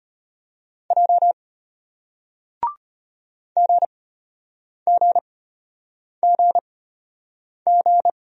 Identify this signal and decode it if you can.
Morse code, heard as a keyed tone, about 34 wpm, spaced as for 8 wpm, 700 Hz. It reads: J GGGG